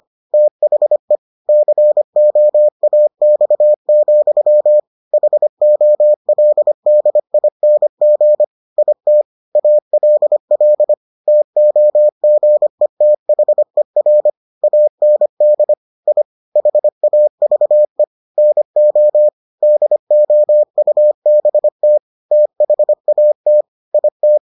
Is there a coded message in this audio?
Yes: Morse code from 0.3 to 24.4 s, heard as a keyed tone, about 25 wpm, 600 Hz. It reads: THE COAX, HOLDING IT ALL TOGETHER AND I HAVE NO DOUBT THAT IT